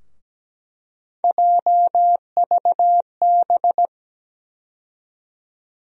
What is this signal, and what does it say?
Morse code, heard as a keyed tone, about 17 wpm, 700 Hz. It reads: JVB
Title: USA Call Sign Postfixes - 3 Letters 17wpm